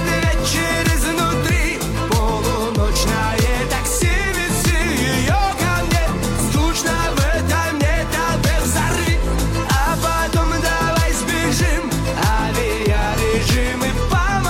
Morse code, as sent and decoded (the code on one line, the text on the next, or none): none